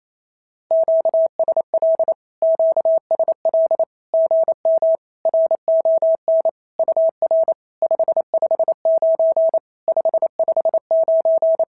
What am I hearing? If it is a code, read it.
Morse code, heard as a keyed tone, about 28 wpm, 650 Hz. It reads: QSL QSL GM RON UR 559 559
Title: SOTA QSO Practice 28wpm